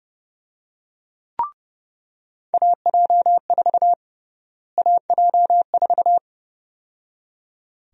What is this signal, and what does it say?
Morse code, heard as a keyed tone, about 30 wpm, 700 Hz. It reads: AJ4 AJ4
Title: USA Call Sign Prefixes - Repeated 2x - 30wpm